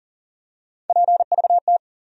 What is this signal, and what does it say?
Morse code, heard as a keyed tone, about 40 wpm, 700 Hz. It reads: PVT
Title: USA Call Sign Postfixes - 3 Letters 40wpm